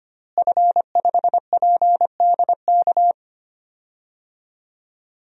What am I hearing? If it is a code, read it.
Morse code, heard as a keyed tone, about 25 wpm, 700 Hz. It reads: F5PDK
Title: DXCC-100 Call Signs - 5 Characters - Repeated 3x - 25wpm